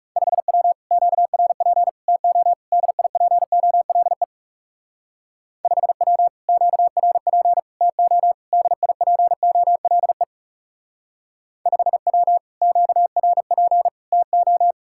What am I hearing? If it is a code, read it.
Morse code, heard as a keyed tone, about 38 wpm, 700 Hz. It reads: 5W QRP TO DIPOLE 5W QRP TO DIPOLE 5W QRP TO